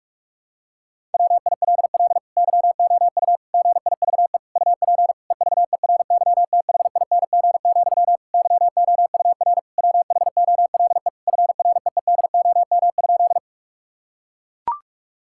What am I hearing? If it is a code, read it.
Morse code, heard as a keyed tone, about 41 wpm, 700 Hz. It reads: WILL YOU GIVE UP EVERYTHING, YOUR WHOLE FREEDOM?
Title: Sentences from Top 1000 Words - Review Part 5 - 45wpm